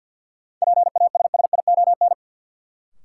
Morse code, heard as a keyed tone, about 53 wpm, 700 Hz. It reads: PASSION
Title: Sentences from Top 1300 Words - New Word Drill Alt - 50wpm